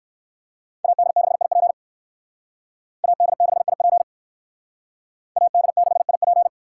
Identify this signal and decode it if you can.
Morse code, heard as a keyed tone, about 59 wpm, 700 Hz. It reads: AD6IP AD6IP AD6IP